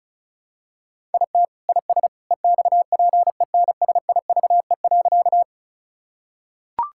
Morse code, heard as a keyed tone, about 36 wpm, 700 Hz. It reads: IT IS EXPENSIVE.
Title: Sentences from Top 1000 Words - Review Part 6 - 35wpm